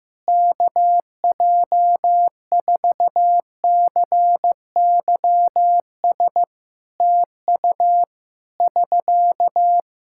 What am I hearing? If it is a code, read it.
Morse code, heard as a keyed tone, about 15 wpm, 700 Hz. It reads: KJ4CYS TU <SK>
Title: Practice QSO - Exchange 4 - Ending - 15wpm